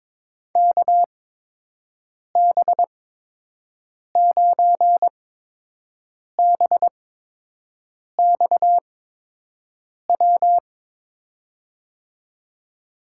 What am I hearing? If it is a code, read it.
Morse code, heard as a keyed tone, about 22 wpm, spaced as for 6 wpm, 700 Hz. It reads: KB9BXW
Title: USA Call Signs - 6 Characters - 8x Character Spacing - 22wpm